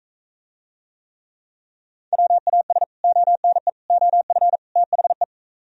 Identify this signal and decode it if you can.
Morse code, heard as a keyed tone, about 43 wpm, 700 Hz. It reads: WAS ONE OF THE